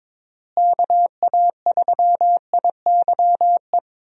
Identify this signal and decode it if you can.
Morse code, heard as a keyed tone, about 22 wpm, 700 Hz. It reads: KA3IYE
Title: USA Call Signs - 6 Characters - Repeated 2x - 22wpm